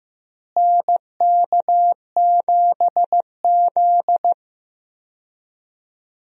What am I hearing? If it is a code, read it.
Morse code, heard as a keyed tone, about 15 wpm, 700 Hz. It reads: NK7Z
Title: USA Call Signs - 4 Characters - Repeated 3x - 15wpm